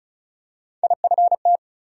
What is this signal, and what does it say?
Morse code, heard as a keyed tone, about 36 wpm, 700 Hz. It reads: IFT
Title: USA Call Sign Postfixes - 3 Letters 35wpm